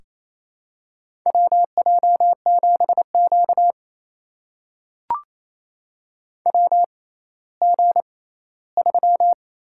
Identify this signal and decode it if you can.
Morse code, heard as a keyed tone, about 28 wpm, spaced as for 16 wpm, 700 Hz. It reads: WJ7Q WG3
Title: USA Call Signs - 4 Characters - 6x Character Spacing - 28wpm